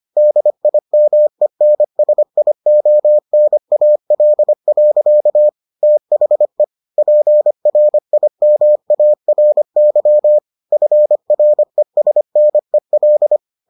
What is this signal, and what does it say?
Morse code, heard as a keyed tone, about 25 wpm, 600 Hz. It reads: DIMENSIONAL. THE PRIMARY FRESNEL